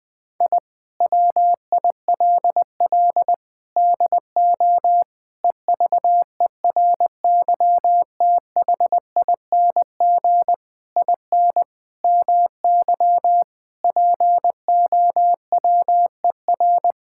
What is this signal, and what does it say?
Morse code, heard as a keyed tone, about 21 wpm, 700 Hz. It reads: I WILL DO EVERYTHING IN MY POWER
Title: Sentences with 8-Word Phrases - Spoken Beginning - 20wpm